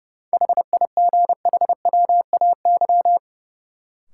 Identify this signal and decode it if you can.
Morse code, heard as a keyed tone, about 30 wpm, 700 Hz. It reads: HIGHWAY
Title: Sentences from Top 1400 Words - New Word Drill - 30wpm